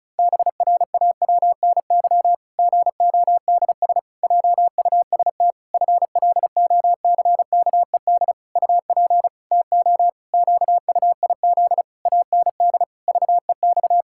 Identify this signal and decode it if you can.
Morse code, heard as a keyed tone, about 34 wpm, 700 Hz. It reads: BRAWNY GODS JUST FLOCKED UP TO QUIZ AND VEX
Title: Pangram Challenge v2 - 35wpm